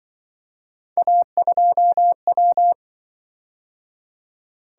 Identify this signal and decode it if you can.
Morse code, heard as a keyed tone, about 25 wpm, 700 Hz. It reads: A2W
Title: Speed-Racing - Three Letter-Number Combinations - 20wpm